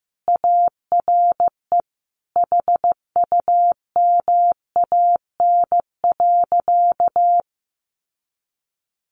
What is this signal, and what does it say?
Morse code, heard as a keyed tone, about 15 wpm, 700 Hz. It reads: ARE HUMAN.
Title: Sentences from Top 600 Words 15wpm